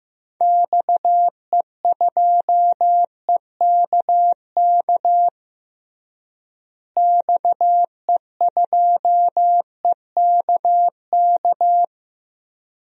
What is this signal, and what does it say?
Morse code, heard as a keyed tone, about 16 wpm, 700 Hz. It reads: XE2EKK XE2EKK
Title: DXCC-100 Call Signs - 6 Characters - Repeated 2x - 15wpm